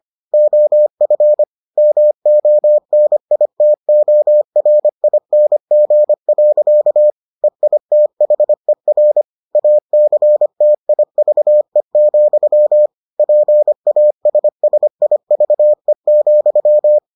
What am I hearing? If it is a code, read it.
Morse code, heard as a keyed tone, about 25 wpm, 600 Hz. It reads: OF MONITORING. EITHER ACTIVE, PASSIVE,